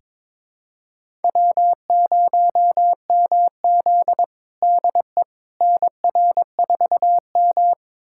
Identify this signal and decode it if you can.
Morse code, heard as a keyed tone, about 22 wpm, 700 Hz. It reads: W0MZ DE NR4M
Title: USA Callsign DE Callsign - 4 Characters - 22wpm